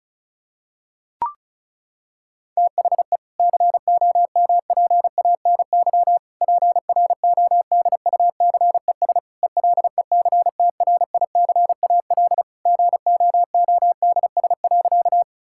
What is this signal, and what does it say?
Morse code, heard as a keyed tone, about 36 wpm, 700 Hz. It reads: THE COMPANY PRODUCES ELECTRICAL GOODS.
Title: Sentences from Top 1000 Words - Review Part 6 - 35wpm